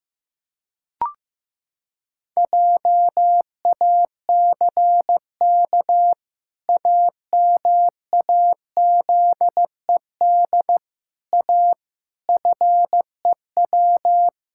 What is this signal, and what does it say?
Morse code, heard as a keyed tone, about 15 wpm, 700 Hz. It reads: JACK AMAZED A FEW
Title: Pangram Challenge v2 - 15wpm